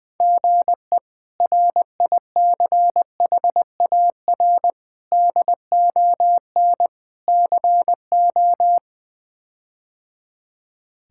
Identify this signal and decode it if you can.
Morse code, heard as a keyed tone, about 21 wpm, 700 Hz. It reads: GE RICHAR DON CO
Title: K1USN SST QSO Practice - 20wpm